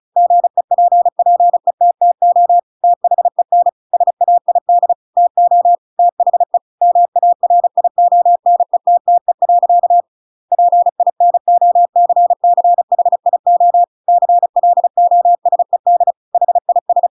Morse code, heard as a keyed tone, about 34 wpm, 700 Hz. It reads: GEPPETTO THEN SAID TO THE MARIONETTE. PINOCCHIO CLOSED HIS
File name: Pinocchio / Pinocchio ch8 35 35 ik1hge